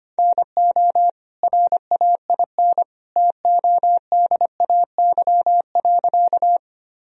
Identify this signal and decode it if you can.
Morse code, heard as a keyed tone, about 25 wpm, 700 Hz. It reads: NO RAIN TODAY.
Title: Sentences from Top 1000 Words - Review Part 1 - 25wpm